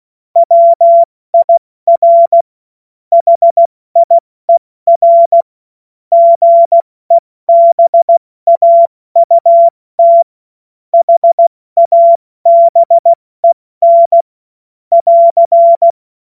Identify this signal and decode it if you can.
Morse code, heard as a keyed tone, about 16 wpm, spaced as for 14 wpm, 680 Hz. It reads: WIR HIER GEBAUT HABEN <AR>
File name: Morsekurs / Witze / Witz 15. August 2023 70 F80